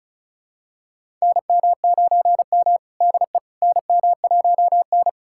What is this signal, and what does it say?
Morse code, heard as a keyed tone, about 36 wpm, 700 Hz. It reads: NM9M DE NM1N
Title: USA Callsign DE Callsign - 4 Characters - 35wpm